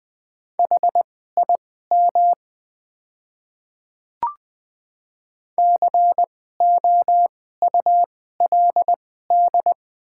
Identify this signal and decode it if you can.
Morse code, heard as a keyed tone, about 20 wpm, spaced as for 14 wpm, 700 Hz. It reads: HIM COULD
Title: 2x Character Spacing - Top 300 Words - 20wpm